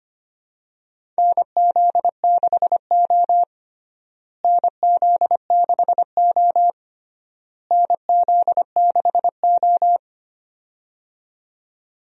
Morse code, heard as a keyed tone, about 25 wpm, 700 Hz. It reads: NZ6O NZ6O NZ6O
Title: USA Call Signs - 4 Characters - Repeated 3x - 25wpm